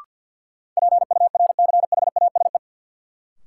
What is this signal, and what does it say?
Morse code, heard as a keyed tone, about 53 wpm, 700 Hz. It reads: PURCHASE
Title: Sentences from Top 800 Words - New Word Drill - 50wpm